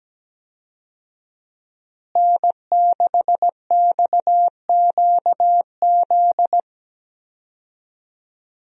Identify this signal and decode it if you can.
Morse code, heard as a keyed tone, about 17 wpm, 700 Hz. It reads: N6XQZ
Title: USA Call Signs - 5 Characters - Repeated 2x - 17wpm